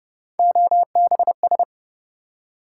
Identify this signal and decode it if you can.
Morse code, heard as a keyed tone, about 31 wpm, 700 Hz. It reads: OBS